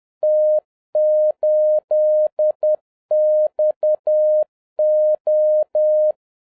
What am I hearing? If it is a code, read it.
Morse code, heard as a keyed tone, about 10 wpm, 610 Hz. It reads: T8XO